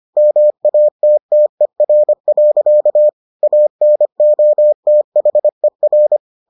Morse code, heard as a keyed tone, about 25 wpm, 600 Hz. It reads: MATTER. ANOTHER